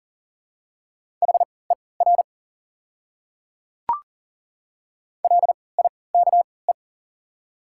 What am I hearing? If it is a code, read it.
Morse code, heard as a keyed tone, about 40 wpm, spaced as for 23 wpm, 700 Hz. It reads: HER LIKE
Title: 3x Character Spacing - Top 100 Words - 40wpm